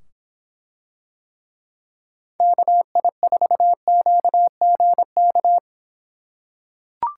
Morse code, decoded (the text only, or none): KI4QGK